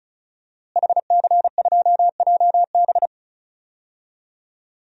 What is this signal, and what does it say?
Morse code, heard as a keyed tone, about 36 wpm, 700 Hz. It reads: HC2JB
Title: DXCC-100 Call Signs - 5 Characters - Repeated 2x - 35wpm